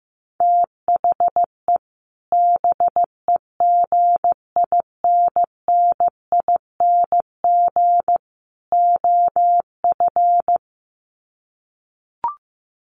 Morse code, heard as a keyed tone, about 15 wpm, 700 Hz. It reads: THE BEGINNING OF
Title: Sets of 3 Words 15wpm